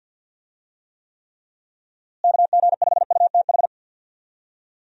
Morse code, heard as a keyed tone, about 52 wpm, 700 Hz. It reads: KG5UTH
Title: USA Call Signs - 6 Characters - Repeated 3x - 50wpm